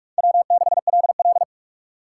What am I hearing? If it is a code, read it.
Morse code, heard as a keyed tone, about 44 wpm, 700 Hz. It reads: W6LL
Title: USA Call Signs - 6x Character Spacing - 45wpm